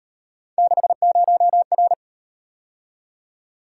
Morse code, heard as a keyed tone, about 37 wpm, 700 Hz. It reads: 60R